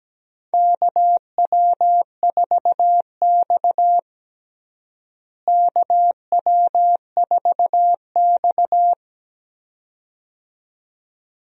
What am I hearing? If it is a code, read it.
Morse code, heard as a keyed tone, about 17 wpm, 700 Hz. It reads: KW4X KW4X
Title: USA Call Signs - 4 Characters - Repeated 2x - 17wpm